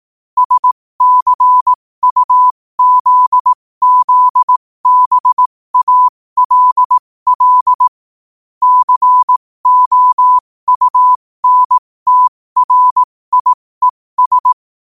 Morse code, heard as a keyed tone, about 18 wpm, spaced as for 15 wpm, 1000 Hz. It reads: SCUZZBALL COUNTRIES
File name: qotdfiles / 15wpm